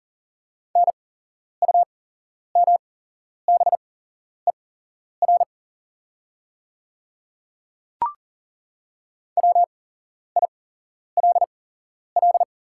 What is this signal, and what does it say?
Morse code, heard as a keyed tone, about 40 wpm, spaced as for 11 wpm, 690 Hz. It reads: NUMBER WILL
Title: 8x Character Spacing - Top 100 Words - 40wpm